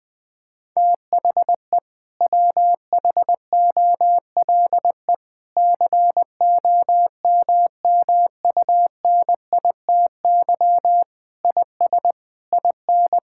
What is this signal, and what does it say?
Morse code, heard as a keyed tone, about 20 wpm, 700 Hz. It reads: THE WHOLE COMMUNITY IS IN
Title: Sentences from Top 1600 Words 20wpm